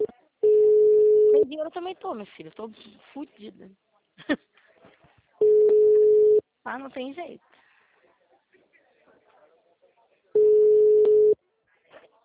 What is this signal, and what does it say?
Ringback tone (Europe, 425 Hz), with speech.